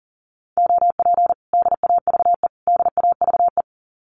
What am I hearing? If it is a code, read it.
Morse code, heard as a keyed tone, about 37 wpm, 700 Hz. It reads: OP DAVE DAVE
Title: Common QSO Phrases Encore 40wpm